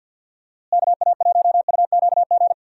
Morse code, heard as a keyed tone, about 50 wpm, 700 Hz. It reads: KA1UQG
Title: USA Call Signs - 6 Characters - Repeated 3x - 50wpm